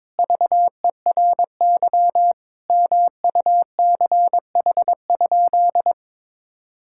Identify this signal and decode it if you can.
Morse code, heard as a keyed tone, about 22 wpm, 700 Hz. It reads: VERY MUCH?
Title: Sentences from Top 1000 Words - Review Part 5 - 22wpm